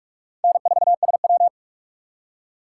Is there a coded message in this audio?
Yes: Morse code from 0.4 to 1.5 s, heard as a keyed tone, about 44 wpm, 700 Hz. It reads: N4SW